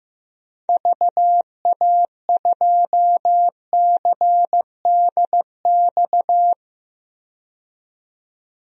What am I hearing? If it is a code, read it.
Morse code, heard as a keyed tone, about 15 wpm, 700 Hz. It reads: VA2CDX
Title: DXCC-100 Call Signs - 6 Characters - 15wpm